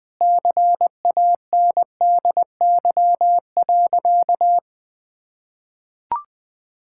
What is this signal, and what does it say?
Morse code, heard as a keyed tone, about 20 wpm, 700 Hz. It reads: CANDY.
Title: Sentences with 4-Word Phrases - Spoken Beginning - 20wpm